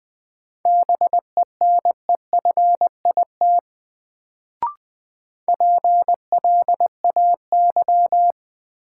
Morse code, heard as a keyed tone, about 20 wpm, 700 Hz. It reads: BENEFIT PLAY
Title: Sets of 5 Words Encore - New Word Drill - 20wpm